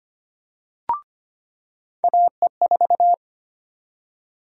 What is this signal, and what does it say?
Morse code, heard as a keyed tone, about 25 wpm, 700 Hz. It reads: AE4
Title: USA Call Sign Prefixes - 25wpm